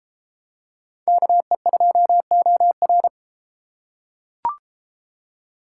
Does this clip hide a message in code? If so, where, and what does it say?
Yes: Morse code from 1.1 to 3.1 s, heard as a keyed tone, about 33 wpm, 700 Hz. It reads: KE2OR